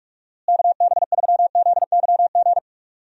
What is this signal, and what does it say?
Morse code, heard as a keyed tone, about 44 wpm, 700 Hz. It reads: KB3ZYG
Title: USA Call Signs - 6 Characters - Repeated 3x - 45wpm